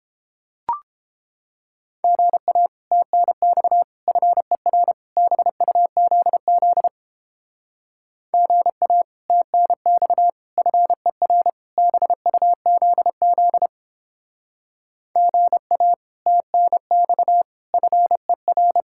Morse code, heard as a keyed tone, about 29 wpm, 700 Hz. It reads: GA TNX FER BUZZ GA TNX FER BUZZ GA TNX FER